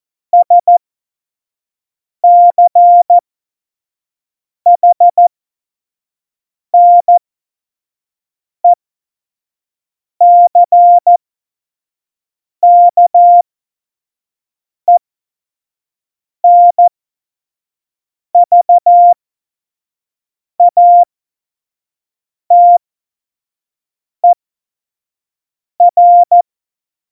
Morse code, heard as a keyed tone, about 14 wpm, spaced as for 5 wpm, 700 Hz. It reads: SCHNECKENVATER
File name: Morsekurs / Witze / Witz 17 April 2018, 25 F70